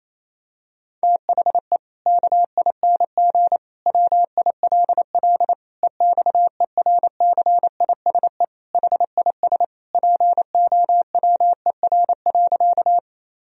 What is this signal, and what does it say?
Morse code, heard as a keyed tone, about 28 wpm, 700 Hz. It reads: THE KING WILL EXERCISE HIS POWER.
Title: Sentences from Top 1000 Words - Review Part 4 - 28wpm